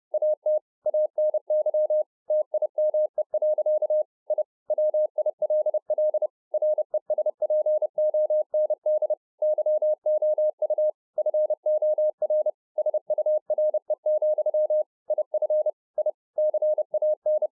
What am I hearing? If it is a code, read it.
Morse code, heard as a keyed tone, about 30 wpm, 610 Hz. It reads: AT ANY TIME. I WILL RESPOND YOU FOR SURE, IF I CAN